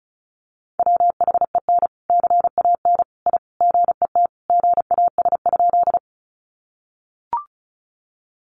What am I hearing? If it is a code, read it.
Morse code, heard as a keyed tone, about 34 wpm, 700 Hz. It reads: WHEN CAN I GET GAS?
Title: Sentences from Top 1100 Words 35wpm